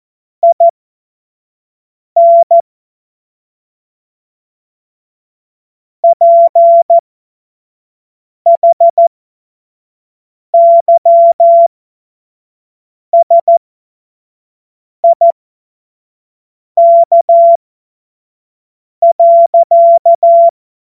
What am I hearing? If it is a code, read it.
Morse code, heard as a keyed tone, about 14 wpm, spaced as for 5 wpm, 680 Hz. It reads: IN PHYSIK.
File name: Morsekurs / Witze / Witz 17. Mai 2022 25 F70